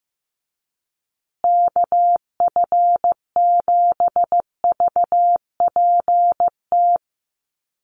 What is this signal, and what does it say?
Morse code, heard as a keyed tone, about 15 wpm, 700 Hz. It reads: KF7VPT
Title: USA Call Signs Repeated 3x 15wpm